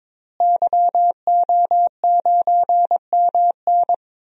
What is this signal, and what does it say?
Morse code, heard as a keyed tone, about 21 wpm, 700 Hz. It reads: YO9MN
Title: DXCC-100 Call Signs - 5 Characters - Repeated 3x - 22wpm